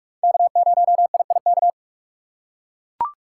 Morse code, heard as a keyed tone, about 41 wpm, 700 Hz. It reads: K0IIK